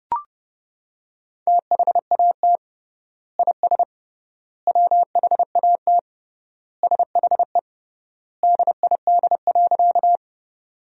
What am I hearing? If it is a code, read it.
Morse code, heard as a keyed tone, about 30 wpm, 700 Hz. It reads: THAT IS WHAT SHE DID.
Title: 3x Word-Spacing - Sentences from Top 100 Words - 30wpm